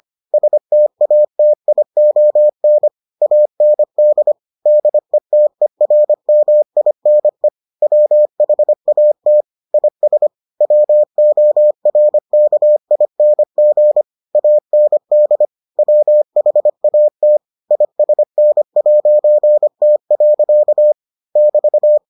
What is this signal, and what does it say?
Morse code, heard as a keyed tone, about 25 wpm, 600 Hz. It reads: STATION AND DETERMINE WHAT IS WORKING AND WHAT ISN'T. <BT>